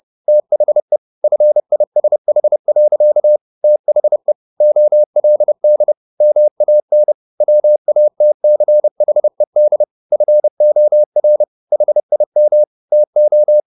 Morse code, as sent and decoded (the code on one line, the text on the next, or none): - .... / ..-. .. ... .... .-.-.- / - .... / --- .-.. -.. / -- .- -. / .-- .- - -.-. .... -.. / ..-. --- .-. / .... .. -- / - ---
THE FISH. THE OLD MAN WATCHED FOR HIM TO